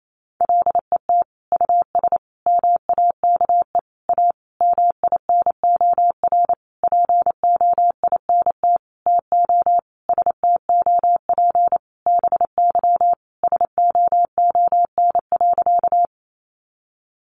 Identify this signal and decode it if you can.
Morse code, heard as a keyed tone, about 28 wpm, 700 Hz. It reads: LET US MAKE A MINOR POINT TO STOP BY SOON.
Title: Sentences from Top 1500 Words 28wpm